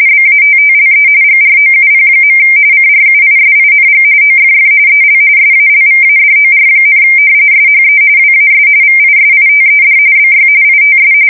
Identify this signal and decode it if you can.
RTTY, about 45.45 baud. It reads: BATTLE GROUP. CRUISING OFF EL SALVADOR, SHE WAS VISITED BY ITS
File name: fsk170 / battleshipnj